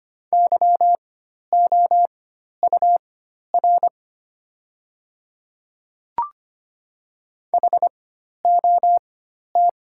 Morse code, heard as a keyed tone, about 25 wpm, spaced as for 12 wpm, 700 Hz. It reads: YOUR HOT
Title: 4x Character Spacing - Top 100 Words - 25wpm